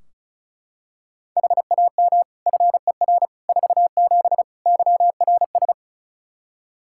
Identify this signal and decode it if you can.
Morse code, heard as a keyed tone, about 34 wpm, 700 Hz. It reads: HAM FER 47 YRS
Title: K1USN SST QSO Practice 35wpm